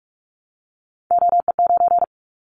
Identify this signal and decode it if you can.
Morse code, heard as a keyed tone, about 46 wpm, 700 Hz. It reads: OE9
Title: Callsign Prefixes for Top 50 DXCC - 45wpm